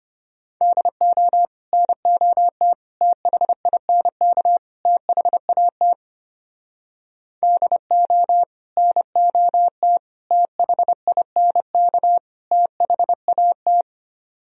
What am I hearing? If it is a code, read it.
Morse code, heard as a keyed tone, about 33 wpm, 700 Hz. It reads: DO NOT THINK THAT DO NOT THINK THAT